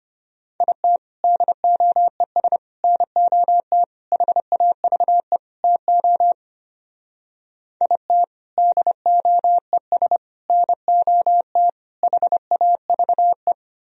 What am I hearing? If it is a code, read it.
Morse code, heard as a keyed tone, about 32 wpm, 700 Hz. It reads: IT DOES NOT HAVE TO IT DOES NOT HAVE